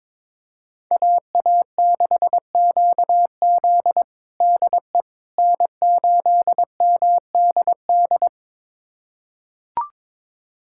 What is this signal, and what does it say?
Morse code, heard as a keyed tone, about 21 wpm, 700 Hz. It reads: AA6QZ DE N8MDD
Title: USA Callsign DE Callsign - 5 Characters - 22wpm